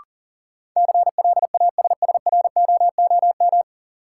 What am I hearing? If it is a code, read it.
Morse code, heard as a keyed tone, about 40 wpm, 700 Hz. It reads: CLASSROOM
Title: Sentences from Top 1500 Words - New Word Drill Alt - 40wpm